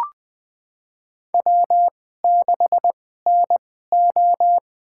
Morse code, heard as a keyed tone, about 20 wpm, spaced as for 15 wpm, 700 Hz. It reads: W6NO
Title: USA Call Signs - 4 Characters - 2x Character Spacing - 20wpm